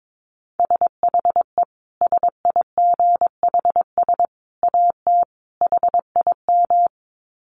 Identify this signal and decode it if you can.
Morse code, heard as a keyed tone, about 22 wpm, 700 Hz. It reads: SHE SIGHS AT HIM
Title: Dit Challenge 22wpm